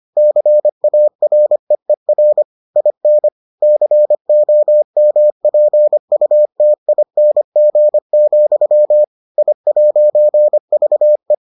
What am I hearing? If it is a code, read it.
Morse code, heard as a keyed tone, about 25 wpm, 600 Hz. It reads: CAREER IN COMPUTING, I'VE